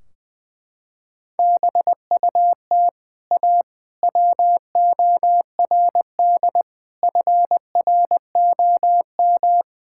Morse code, heard as a keyed tone, about 20 wpm, 700 Hz. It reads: BUT A WORD FROM
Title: Sentences from Top 1600 Words 20wpm